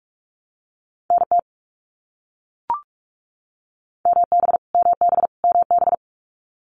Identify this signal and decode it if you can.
Morse code, heard as a keyed tone, about 44 wpm, 700 Hz. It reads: NT MB MB MB
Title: Canadian Provinces and Territories - Repeated 3x - 45wpm